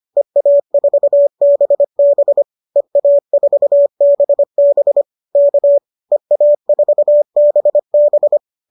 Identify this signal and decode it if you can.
Morse code, heard as a keyed tone, about 25 wpm, 590 Hz. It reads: EA4BB EA4BB K EA4BB